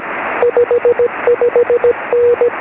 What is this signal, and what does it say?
Morse code, heard as a keyed tone, about 17 wpm, 480 Hz, with radio noise. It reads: 55N
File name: sm7lcb / g4hup 23cm